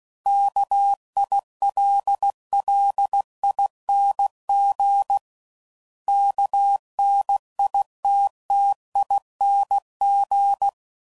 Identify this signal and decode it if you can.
Morse code, heard as a keyed tone, about 16 wpm, 800 Hz. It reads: KILLING KNITTING
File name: Suffixes / Special INGs / Sngl Ings